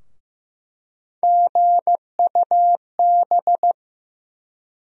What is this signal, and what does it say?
Morse code, heard as a keyed tone, about 15 wpm, 700 Hz. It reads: GUB
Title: USA Call Sign Postfixes - 3 Letters 15wpm